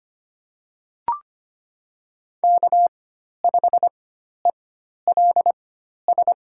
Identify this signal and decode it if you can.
Morse code, heard as a keyed tone, about 25 wpm, spaced as for 12 wpm, 700 Hz. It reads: K5ELS
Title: USA Call Signs - 5 Characters - 4x Character Spacing - 25wpm